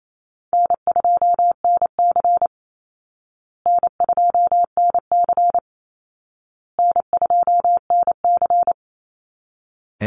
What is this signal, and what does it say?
Morse code, heard as a keyed tone, about 28 wpm, 700 Hz. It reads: N2NC N2NC N2NC
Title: Easy Call Signs Repeated 3x - 28wpm